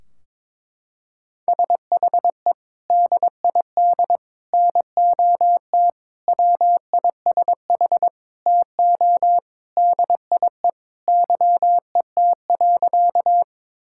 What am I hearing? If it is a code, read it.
Morse code, heard as a keyed tone, about 22 wpm, 700 Hz. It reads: SHE DID NOT WISH TO DIE YET.
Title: Sentences from Top 1600 Words 22wpm